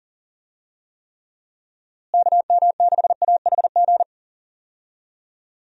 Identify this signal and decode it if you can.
Morse code, heard as a keyed tone, about 38 wpm, 700 Hz. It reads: KM6AHG
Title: USA Call Signs - 6 Characters - Repeated 2x - 40wpm